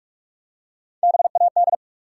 Morse code, heard as a keyed tone, about 46 wpm, 700 Hz. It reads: BAD
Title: Three Letter Words - 45wpm